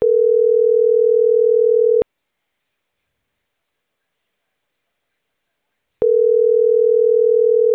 Ringback tone (North America), 440 + 480 Hz.